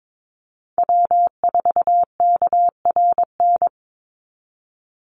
Morse code, heard as a keyed tone, about 22 wpm, 700 Hz. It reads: W4KRN
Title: NA SOTA Chaser Callsigns 22wpm